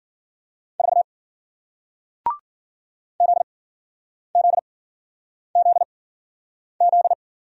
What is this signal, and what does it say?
Morse code, heard as a keyed tone, about 55 wpm, spaced as for 9 wpm, 700 Hz. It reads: 4 ZZZZ